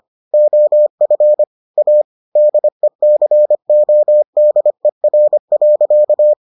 Morse code, heard as a keyed tone, about 25 wpm, 600 Hz. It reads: OF A DECODER.